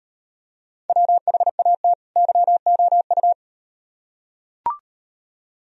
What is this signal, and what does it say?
Morse code, heard as a keyed tone, about 36 wpm, 700 Hz. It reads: WHAT YOU